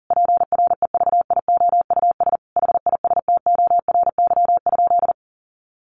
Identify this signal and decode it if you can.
Morse code, heard as a keyed tone, about 40 wpm, 700 Hz. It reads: PREVIOUS HISTORY?
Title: Sentences from Top 1500 Words 40wpm